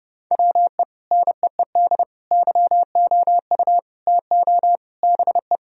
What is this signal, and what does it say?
Morse code, heard as a keyed tone, about 32 wpm, 700 Hz. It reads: WE NEED YOU TO BE
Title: Sentences from Top 1000 Words - Review Part 3 - 30wpm